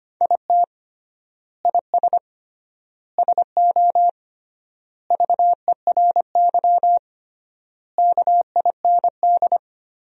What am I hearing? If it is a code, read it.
Morse code, heard as a keyed tone, about 25 wpm, 700 Hz. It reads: IT IS SO VERY KIND
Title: 3x Word-Spacing - Sentences from Top 300 Words - 25wpm